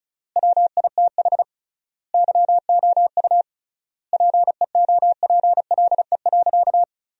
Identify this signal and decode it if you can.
Morse code, heard as a keyed tone, about 36 wpm, 700 Hz. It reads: WITH YOU PEOPLE.
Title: 3x Word-Spacing - Sentences from Top 300 Words - 35wpm